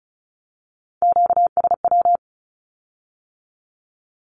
Morse code, heard as a keyed tone, about 36 wpm, 700 Hz. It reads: QSW
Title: Q-Codes with Spoken Meaning - 35wpm